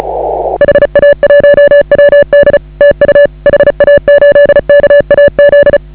Morse code, heard as a keyed tone, about 36 wpm, 1700 Hz, with noise. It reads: HA1WD TU HA8KAZ
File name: AUDIO2019 / HA8KAZ CW